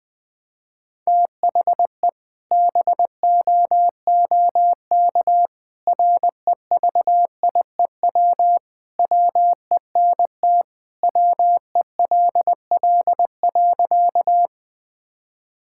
Morse code, heard as a keyed tone, about 20 wpm, 700 Hz. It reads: THE BOOK REVIEW WENT WELL.
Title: Sentences from Top 1000 Words - Review Part 6 - 20wpm